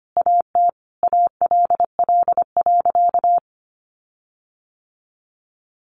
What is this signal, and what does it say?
Morse code, heard as a keyed tone, about 25 wpm, 700 Hz. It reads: AT ALL.
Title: Sentences from Top 1300 Words 25wpm